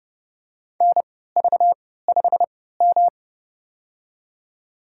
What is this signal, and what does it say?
Morse code, heard as a keyed tone, about 30 wpm, spaced as for 17 wpm, 700 Hz. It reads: NV5M
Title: USA Call Signs - 4 Characters - 3x Character Spacing - 30wpm